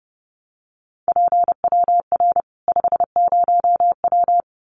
Morse code, heard as a keyed tone, about 30 wpm, 700 Hz. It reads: PWR 50W
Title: Common QSO Phrases 30wpm